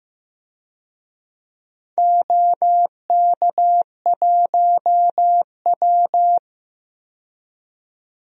Morse code, heard as a keyed tone, about 15 wpm, 700 Hz. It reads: OK1W
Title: DXCC-100 Call Signs - 4 Characters - 15wpm